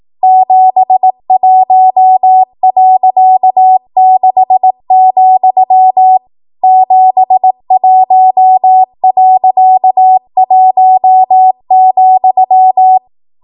Morse code, heard as a keyed tone, about 18 wpm, 750 Hz. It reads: 71.6, 71.1,
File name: Archive / 18 WPM / 18WPM